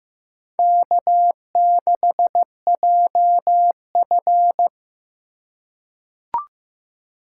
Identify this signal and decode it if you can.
Morse code, heard as a keyed tone, about 15 wpm, 700 Hz. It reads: K6JF
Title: USA Call Signs - 4 Characters - 4x Character Spacing - 15wpm